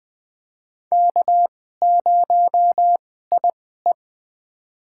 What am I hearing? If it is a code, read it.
Morse code, heard as a keyed tone, about 20 wpm, spaced as for 15 wpm, 700 Hz. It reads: K0IE
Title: USA Call Signs - 2x Character Spacing - 20wpm